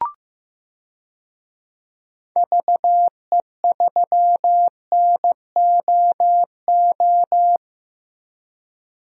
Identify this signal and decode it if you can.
Morse code, heard as a keyed tone, about 15 wpm, 700 Hz. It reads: VE3NOO
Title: DXCC-100 Call Signs - 6 Characters - Repeated 2x - 15wpm